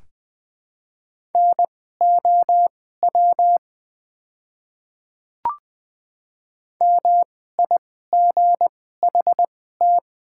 Morse code, heard as a keyed tone, about 20 wpm, spaced as for 14 wpm, 700 Hz. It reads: NOW MIGHT
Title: 2x Character Spacing - Top 300 Words - 20wpm